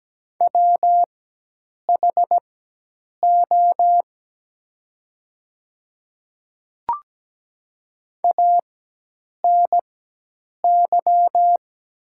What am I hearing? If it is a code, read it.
Morse code, heard as a keyed tone, about 17 wpm, spaced as for 8 wpm, 700 Hz. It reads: WHO ANY
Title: 4x Character Spacing - Top 300 Words - 17wpm